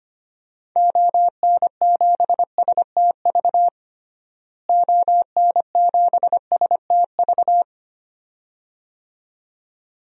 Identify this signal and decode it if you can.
Morse code, heard as a keyed tone, about 25 wpm, 700 Hz. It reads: ON7STV ON7STV
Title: DXCC-100 Call Signs - 6 Characters - Repeated 2x - 25wpm